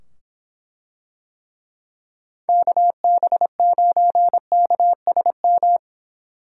Morse code, heard as a keyed tone, about 25 wpm, 700 Hz. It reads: KB9KSM